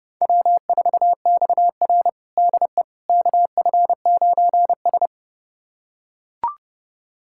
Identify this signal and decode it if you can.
Morse code, heard as a keyed tone, about 30 wpm, 700 Hz. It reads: W4XR DE KF9S
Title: USA Callsign DE Callsign - 4 Characters - 30wpm